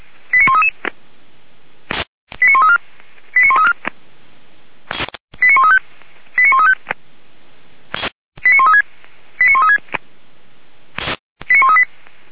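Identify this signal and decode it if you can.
Five-tone selective call (ZVEI), heard as quick runs of five tones, tones of about 70 ms. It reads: 89133 89134 89134 89135 89135 89136 89136 89138
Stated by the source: ZVEI five-tone selective call received on SDR.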